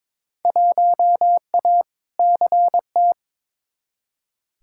Morse code, heard as a keyed tone, about 21 wpm, 700 Hz. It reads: 1A CT
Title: Field Day Exchanges Encore - 22wpm